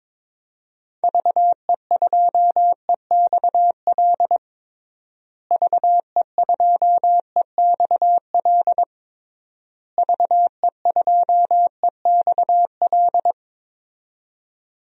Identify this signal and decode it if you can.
Morse code, heard as a keyed tone, about 22 wpm, 700 Hz. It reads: VE2EXL VE2EXL VE2EXL
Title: DXCC-100 Call Signs - 6 Characters - Repeated 3x - 22wpm